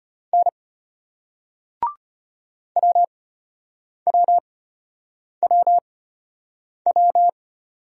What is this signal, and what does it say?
Morse code, heard as a keyed tone, about 34 wpm, spaced as for 8 wpm, 700 Hz. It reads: N WWWW